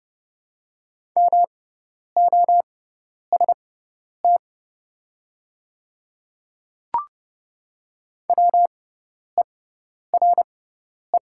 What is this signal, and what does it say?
Morse code, heard as a keyed tone, about 30 wpm, spaced as for 10 wpm, 700 Hz. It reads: MOST WERE
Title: 6x Character Spacing - Top 100 Words - 30wpm